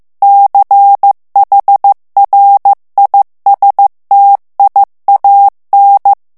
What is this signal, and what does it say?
Morse code, heard as a keyed tone, about 15 wpm, 800 Hz. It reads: CHRISTIAN
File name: morse mp3 / ALLEMAND texte clair 1 vitesse 14 mots minute